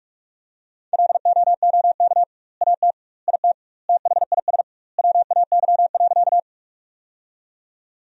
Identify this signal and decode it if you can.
Morse code, heard as a keyed tone, about 46 wpm, 700 Hz. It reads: LOOK AT IT THIS WAY.
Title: 2x Word-Spacing - Sentences from Top 100 Words - 45wpm